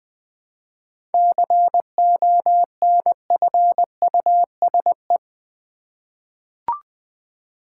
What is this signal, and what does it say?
Morse code, heard as a keyed tone, about 20 wpm, 700 Hz. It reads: CONFUSE
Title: 7 Letter Words - First 3 Letters Given - 20wpm